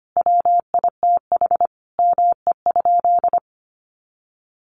Morse code, heard as a keyed tone, about 25 wpm, 700 Hz. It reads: WITH ME?
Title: Sentences from Top 800 Words 25wpm